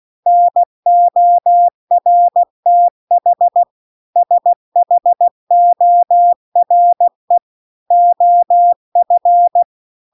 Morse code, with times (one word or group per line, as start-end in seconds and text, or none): 0.3-3.6 NORTH
4.2-7.4 SHORE
7.9-9.6 OF